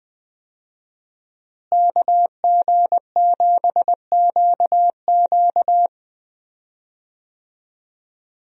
Morse code, heard as a keyed tone, about 20 wpm, 700 Hz. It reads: KG7QQ
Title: USA Call Signs - 5 Characters - Repeated 2x - 20wpm